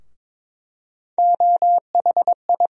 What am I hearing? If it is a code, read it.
Morse code, heard as a keyed tone, about 22 wpm, 700 Hz. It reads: OHI
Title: USA Call Sign Postfixes - 3 Letters 22wpm